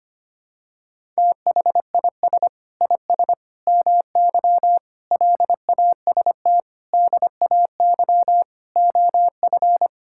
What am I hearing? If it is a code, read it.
Morse code, heard as a keyed tone, about 25 wpm, 700 Hz. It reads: THIS IS MY LAST DAY OF
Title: Sentences from Top 1000 Words - Review Part 5 - 25wpm